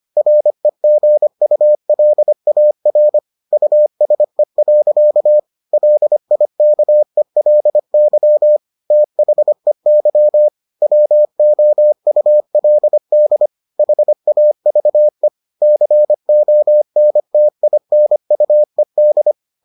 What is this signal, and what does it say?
Morse code, heard as a keyed tone, about 25 wpm, 600 Hz. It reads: REGULAR USE. LIKELY THEY WOULD HAVE CONTINUED